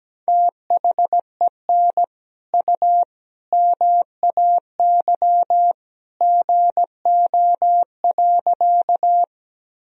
Morse code, heard as a keyed tone, about 17 wpm, 700 Hz. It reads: THEN U MAY GO.
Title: Sentences from Top 100 Words and Abbreviations 17wpm